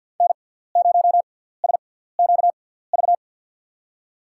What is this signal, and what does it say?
Morse code, heard as a keyed tone, about 52 wpm, spaced as for 17 wpm, 700 Hz. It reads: N0SQV
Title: USA Call Signs - 5 Characters - 6x Character Spacing - 50wpm